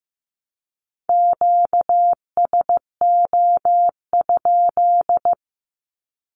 Morse code, heard as a keyed tone, about 15 wpm, 700 Hz. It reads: QSO?
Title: Q-Codes 15wpm